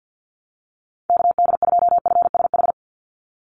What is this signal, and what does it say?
Morse code, heard as a keyed tone, about 50 wpm, 700 Hz. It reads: KD2FSH
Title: Call Signs Repeated 3x - 50wpm